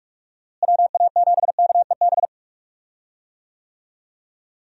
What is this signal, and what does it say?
Morse code, heard as a keyed tone, about 46 wpm, 700 Hz. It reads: WA7KEB
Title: USA Call Signs - 6 Characters - Repeated 2x - 45wpm